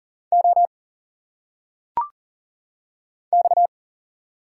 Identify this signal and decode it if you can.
Morse code, heard as a keyed tone, about 40 wpm, 700 Hz. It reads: O X
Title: Single Letters - Phonetic Pronunciation 40wpm